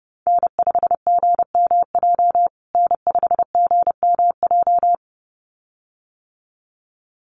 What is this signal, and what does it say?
Morse code, heard as a keyed tone, about 30 wpm, 700 Hz. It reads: N5GMJ N5GMJ
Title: USA Call Signs Repeated 2x 30wpm